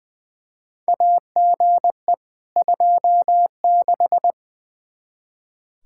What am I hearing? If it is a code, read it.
Morse code, heard as a keyed tone, about 21 wpm, 700 Hz. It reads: AGE 26
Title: Practice QSO - Exchange 3 - More Info - 20wpm